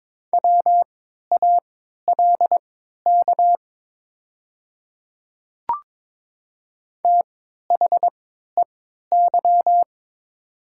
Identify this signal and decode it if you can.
Morse code, heard as a keyed tone, about 22 wpm, spaced as for 12 wpm, 700 Hz. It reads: WALK THEY
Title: 3x Character Spacing - Top 300 Words - 22wpm